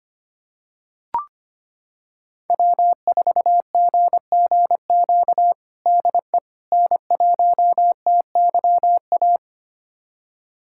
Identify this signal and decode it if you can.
Morse code, heard as a keyed tone, about 25 wpm, 700 Hz. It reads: W4GGQ DE N1TYA
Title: USA Callsign DE Callsign - 5 Characters - 25wpm